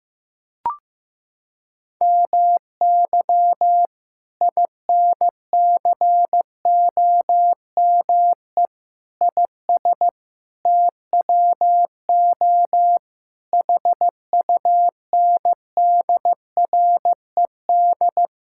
Morse code, heard as a keyed tone, about 15 wpm, 700 Hz. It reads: MY INCOME IS TWO HUNDRED
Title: Sentences from Top 1000 Words - Review Part 5 - 15wpm